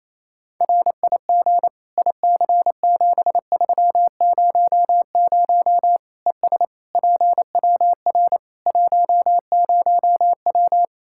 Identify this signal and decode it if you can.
Morse code, heard as a keyed tone, about 28 wpm, 700 Hz. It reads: RIG IC7300 ES PWR 10W
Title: K1USN SST QSO Practice 28wpm